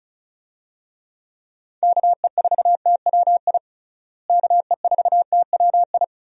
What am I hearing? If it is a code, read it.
Morse code, heard as a keyed tone, about 34 wpm, 700 Hz. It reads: KE4TWI KE4TWI
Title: USA Call Signs - 6 Characters - Repeated 3x - 35wpm